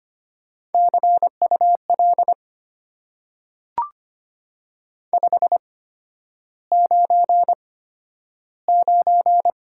Morse code, heard as a keyed tone, about 25 wpm, spaced as for 11 wpm, 700 Hz. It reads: CUL 599
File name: Top 90 QSO Elements - 8x Character Spacing - 25wpm